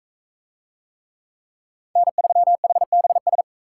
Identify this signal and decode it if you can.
Morse code, heard as a keyed tone, about 40 wpm, 700 Hz. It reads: N3HBS